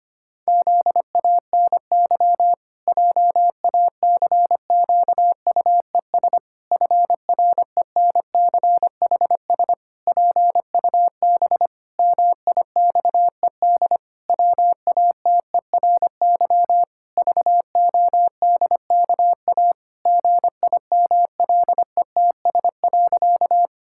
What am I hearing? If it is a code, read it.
Morse code, heard as a keyed tone, about 25 wpm, 700 Hz. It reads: ZANY JACQUES FRENCHS PUB MIXED WATERY VODKA GIMLETS.
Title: Pangram Challenge v2 - 25wpm